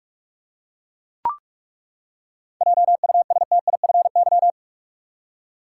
Morse code, heard as a keyed tone, about 42 wpm, 700 Hz. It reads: JUSTIFY
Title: Seven Letter Words - 45wpm